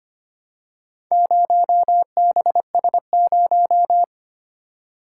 Morse code, heard as a keyed tone, about 25 wpm, 700 Hz. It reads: 0BS0
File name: practice / 25wpm / Four Letter-Number Combinations - 25wpm